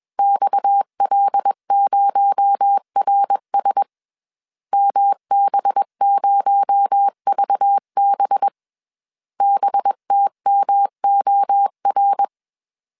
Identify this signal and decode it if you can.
Morse code, heard as a keyed tone, about 21 wpm, 800 Hz. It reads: XL0RS MB0VB BTMOR
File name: CW / DET / 1 6 1500C S A 20wpm 2048